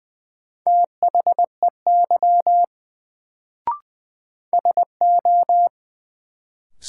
Morse code, heard as a keyed tone, about 20 wpm, 700 Hz. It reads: THEY SO
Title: Sets of 2 Words - New Word Drill - 20wpm